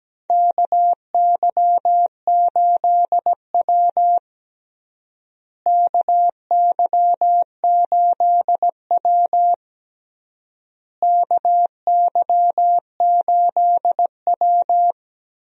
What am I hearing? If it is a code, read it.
Morse code, heard as a keyed tone, about 17 wpm, 700 Hz. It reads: KY8W KY8W KY8W
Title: USA Call Signs - 4 Characters - Repeated 3x - 17wpm